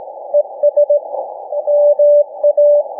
Morse code, heard as a keyed tone, about 18 wpm, 590 Hz. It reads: ES MA
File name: cw / ssn 40M